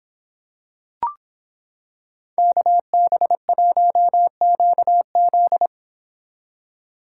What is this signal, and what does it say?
Morse code, heard as a keyed tone, about 26 wpm, 700 Hz. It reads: KB1QZ